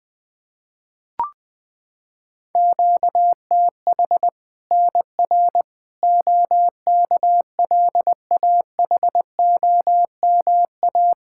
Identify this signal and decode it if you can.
Morse code, heard as a keyed tone, about 20 wpm, 700 Hz. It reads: QTH NR OKLAHOMA
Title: Practice QSO - Exchange 1 - Beginning - 20wpm